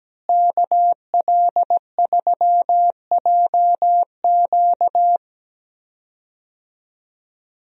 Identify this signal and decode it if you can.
Morse code, heard as a keyed tone, about 17 wpm, 700 Hz. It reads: KL3JQ
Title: USA Call Signs - 5 Characters - Repeated 2x - 17wpm